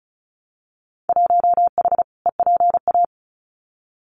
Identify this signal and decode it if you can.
Morse code, heard as a keyed tone, about 36 wpm, 690 Hz. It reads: 1H EPA
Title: Winter Field Day Exchanges 35wpm